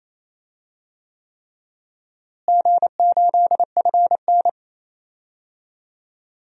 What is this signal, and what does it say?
Morse code, heard as a keyed tone, about 28 wpm, 700 Hz. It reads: G8FN
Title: DXCC-100 Call Signs - 4 Characters - Repeated 3x - 28wpm